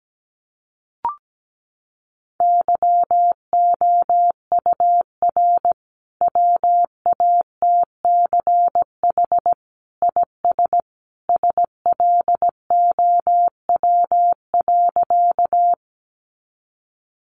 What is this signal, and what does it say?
Morse code, heard as a keyed tone, about 17 wpm, 700 Hz. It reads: YOUR WATCH IS SLOW.
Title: Sentences from Top 400 Words 17wpm